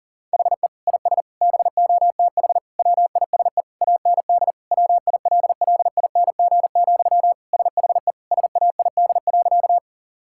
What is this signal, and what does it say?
Morse code, heard as a keyed tone, about 40 wpm, 700 Hz. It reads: HE IS BOTH WISE AND WILLING, SHE SAID.
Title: Sentences from Top 1600 Words 40wpm